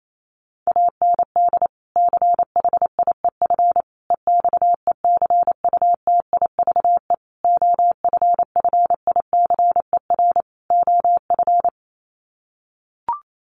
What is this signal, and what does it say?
Morse code, heard as a keyed tone, about 28 wpm, 700 Hz. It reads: AND CHIEF EXECUTIVE OFFICER OF
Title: Sets of 5 Words Encore2 28wpm